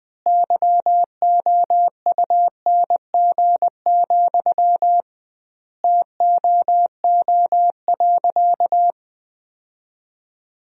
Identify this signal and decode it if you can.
Morse code, heard as a keyed tone, about 20 wpm, 700 Hz. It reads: YOUNG, TOO.
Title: 2x Word-Spacing - Sentences from Top 400 Words - 20wpm